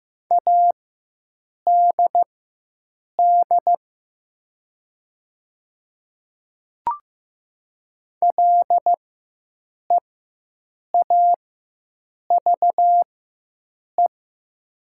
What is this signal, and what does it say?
Morse code, heard as a keyed tone, about 15 wpm, spaced as for 7 wpm, 700 Hz. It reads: ADD LEAVE
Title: 4x Character Spacing - Top 500 Words - 15wpm